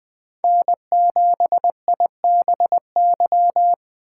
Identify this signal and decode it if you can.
Morse code, heard as a keyed tone, about 19 wpm, 700 Hz. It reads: N7IBY